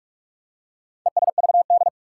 Morse code, heard as a keyed tone, about 43 wpm, 700 Hz. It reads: ESVD
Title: Four Challenging Letter-Number Combinations 45wpm